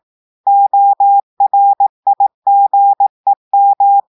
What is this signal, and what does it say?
Morse code, heard as a keyed tone, about 18 wpm, 800 Hz. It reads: ORIGEM
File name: audios / portugais / portugais texte en clair 1 vitesse 18 mots minutes